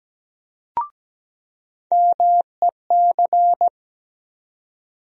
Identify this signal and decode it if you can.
Morse code, heard as a keyed tone, about 17 wpm, 700 Hz. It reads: MEC
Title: USA Call Sign Postfixes - 3 Letters 17wpm